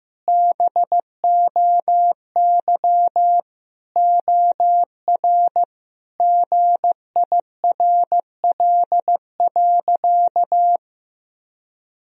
Morse code, heard as a keyed tone, about 15 wpm, 700 Hz. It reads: BOY OR GIRL.
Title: Sentences from Top 1000 Words - Review Part 3 - 15wpm